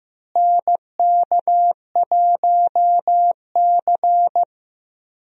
Morse code, heard as a keyed tone, about 15 wpm, 700 Hz. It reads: NK1C
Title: USA Call Signs - 4 Characters - Repeated 2x - 15wpm